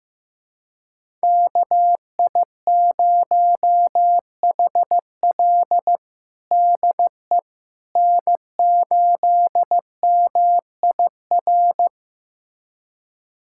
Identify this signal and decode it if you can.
Morse code, heard as a keyed tone, about 16 wpm, 700 Hz. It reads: KI0HL DE N8MIR
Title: USA Callsign DE Callsign - 5 Characters - 15wpm